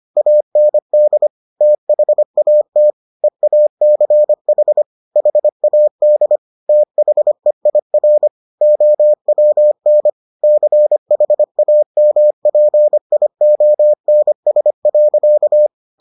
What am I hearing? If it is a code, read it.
Morse code, heard as a keyed tone, about 25 wpm, 600 Hz. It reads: AND THAT EACH HAD THEIR OWN CHAMPIONS.